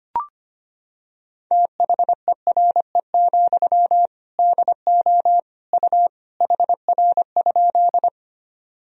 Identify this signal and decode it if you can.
Morse code, heard as a keyed tone, about 25 wpm, 700 Hz. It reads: THERE, DO U HR?
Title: Sentences from Top 300 Words and Abbreviations 25wpm